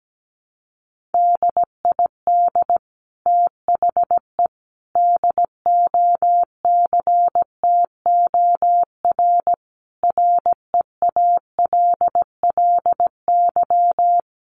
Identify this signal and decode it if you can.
Morse code, heard as a keyed tone, about 17 wpm, 700 Hz. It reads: DID THE DOCTOR REALLY
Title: Sentences from Top 1500 Words 17wpm